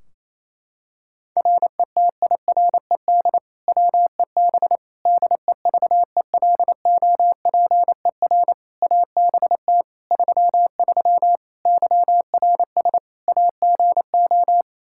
Morse code, heard as a keyed tone, about 28 wpm, 700 Hz. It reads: RETIRED WEB DEVELOPER ABT 33 YRS AGO
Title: K1USN SST QSO Practice 28wpm